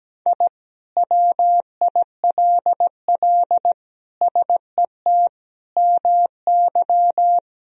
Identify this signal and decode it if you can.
Morse code, heard as a keyed tone, about 17 wpm, 700 Hz. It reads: I WILL SET MY
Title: Sentences from Top 1000 Words - Review Part 2 - 17wpm